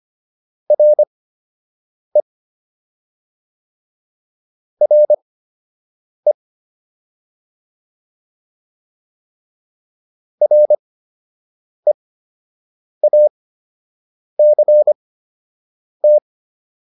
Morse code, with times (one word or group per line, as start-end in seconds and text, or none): 0.7-2.2 RE
4.8-6.3 RE
10.4-16.2 REACT